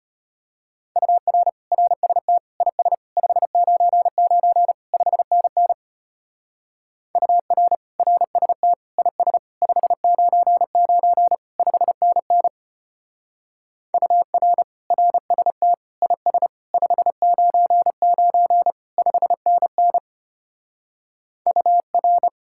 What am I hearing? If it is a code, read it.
Morse code, heard as a keyed tone, about 31 wpm, 700 Hz. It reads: UR RST IS 599 5NN UR RST IS 599 5NN UR RST IS 599 5NN UR
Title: Speed-Racing Common QSO Phrases Encore - 25wpm